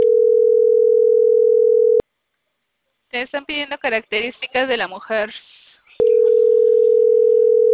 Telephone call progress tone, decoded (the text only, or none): ringback tone